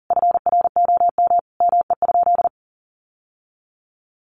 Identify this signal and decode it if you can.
Morse code, heard as a keyed tone, about 40 wpm, 700 Hz. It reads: FROM ME?
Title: Sentences from Top 1500 Words 40wpm